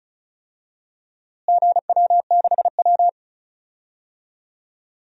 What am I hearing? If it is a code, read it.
Morse code, heard as a keyed tone, about 34 wpm, 700 Hz. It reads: GW6W
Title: DXCC-100 Call Signs - 4 Characters - Repeated 2x - 35wpm